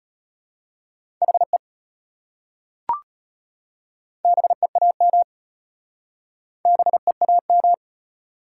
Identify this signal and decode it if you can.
Morse code, heard as a keyed tone, about 32 wpm, 700 Hz. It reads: HE BEAM BEAM